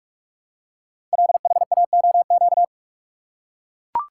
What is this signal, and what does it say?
Morse code, heard as a keyed tone, about 44 wpm, 700 Hz. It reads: LHAOQ
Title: Code Groups - 5 Letters - 45wpm